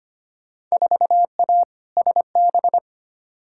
Morse code, heard as a keyed tone, about 25 wpm, 700 Hz. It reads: 4A SB
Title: Field Day Exchanges Encore - 25wpm